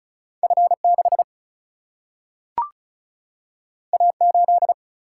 Morse code, heard as a keyed tone, about 36 wpm, 700 Hz. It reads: F 6 A 8
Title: Single-Letter Single-Number 35wpm